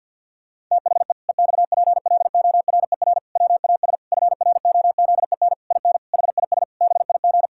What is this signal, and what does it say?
Morse code, heard as a keyed tone, about 52 wpm, 700 Hz. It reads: THE EXPLORER WAS FROZEN IN HIS BIG
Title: Pangram Challenge v2 - 50wpm